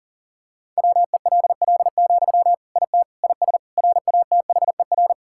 Morse code, heard as a keyed tone, about 40 wpm, 700 Hz. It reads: WELL, IT IS RATHER
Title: Sentences from Top 1600 Words 40wpm